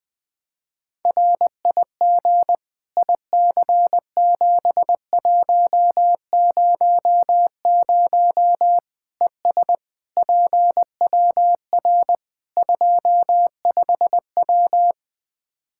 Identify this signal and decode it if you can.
Morse code, heard as a keyed tone, about 20 wpm, 700 Hz. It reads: RIG IC7100 ES PWR 25W
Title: K1USN SST QSO Practice 20wpm